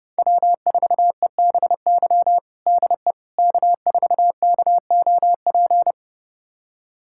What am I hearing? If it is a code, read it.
Morse code, heard as a keyed tone, about 30 wpm, 700 Hz. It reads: W4EBY DE K4KOP
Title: USA Callsign DE Callsign - 5 Characters - 30wpm